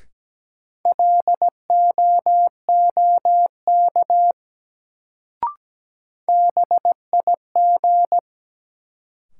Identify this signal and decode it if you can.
Morse code, heard as a keyed tone, about 17 wpm, 700 Hz. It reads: LOOK BIG
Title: Sets of 2 Words - New Word Drill - 17wpm